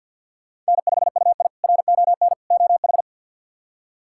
Morse code, heard as a keyed tone, about 53 wpm, 700 Hz. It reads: N5UI RON OH
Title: K1USN SST QSO Practice - 50wpm